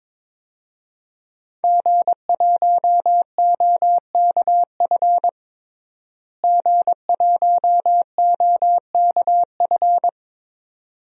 Morse code, heard as a keyed tone, about 22 wpm, 700 Hz. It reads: G1OKF G1OKF
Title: DXCC-100 Call Signs - 5 Characters - Repeated 2x - 22wpm